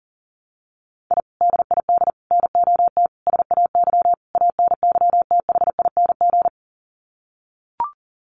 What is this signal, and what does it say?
Morse code, heard as a keyed tone, about 40 wpm, 700 Hz. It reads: I DID NOT SAY ANYTHING
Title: Sets of 5 Words Encore2 40wpm